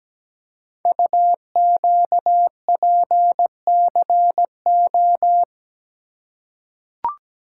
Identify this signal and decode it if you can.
Morse code, heard as a keyed tone, about 17 wpm, 700 Hz. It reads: UQPCO
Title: Code Groups - 5 Letters - 17wpm